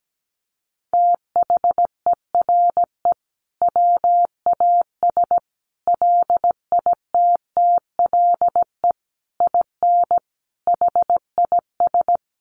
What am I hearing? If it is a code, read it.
Morse code, heard as a keyed tone, about 17 wpm, 700 Hz. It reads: THERE WAS LITTLE IN HIS
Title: Sentences from Top 1400 Words 17wpm